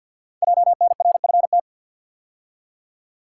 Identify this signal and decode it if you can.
Morse code, heard as a keyed tone, about 52 wpm, 700 Hz. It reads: JNRFT
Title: Code Groups - 5 Letter:Numbers - 50wpm